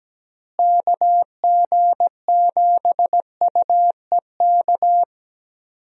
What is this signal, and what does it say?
Morse code, heard as a keyed tone, about 17 wpm, 700 Hz. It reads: KG7UEK
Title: USA Call Signs - 6 Characters - Repeated 3x - 17wpm